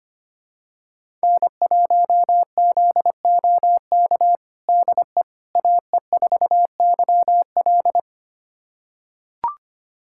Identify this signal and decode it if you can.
Morse code, heard as a keyed tone, about 25 wpm, 700 Hz. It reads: N1ZOK DE AE4YL
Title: USA Callsign DE Callsign - 5 Characters - 25wpm